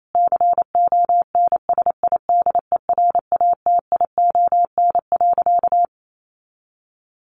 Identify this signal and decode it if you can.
Morse code, heard as a keyed tone, about 28 wpm, 700 Hz. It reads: CONSIDERATION.
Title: Sentences from Top 1500 Words 28wpm